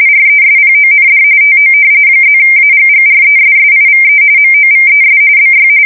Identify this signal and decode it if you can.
RTTY, about 45.45 baud. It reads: RESPOND TO +AGGRESSIVE MOVES BY
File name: fsk170 / usnavy5